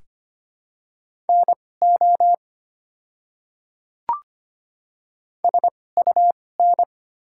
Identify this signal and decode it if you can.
Morse code, heard as a keyed tone, about 25 wpm, spaced as for 18 wpm, 700 Hz. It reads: NO SUN
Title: 2x Character Spacing - Top 300 Words - 25wpm